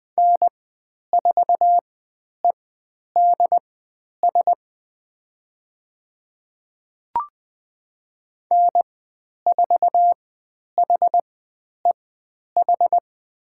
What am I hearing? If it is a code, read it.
Morse code, heard as a keyed tone, about 20 wpm, spaced as for 10 wpm, 700 Hz. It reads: N4EDS N4HEH
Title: 102 Lesson 17g - Letter u - Call Signs 20wpm